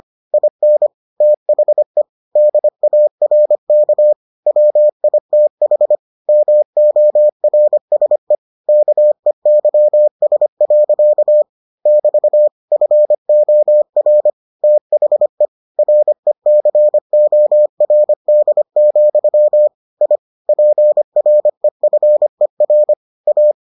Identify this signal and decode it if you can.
Morse code, heard as a keyed tone, about 25 wpm, 600 Hz. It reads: IN THE DARK WITH MORSE KEYS. <BT> FOR THE RECORD, I PREFER A